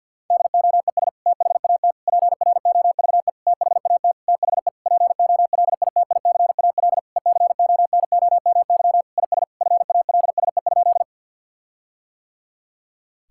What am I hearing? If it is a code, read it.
Morse code, heard as a keyed tone, about 52 wpm, 700 Hz. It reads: DOES THAT PROVE THAT THE POLITICAL ECONOMY IS FALSE?
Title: Sentences from Top 1600 Words 50wpm